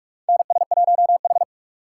Morse code, heard as a keyed tone, about 43 wpm, 690 Hz. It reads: NS1H
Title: USA Call Signs - 3x Character Spacing - 45wpm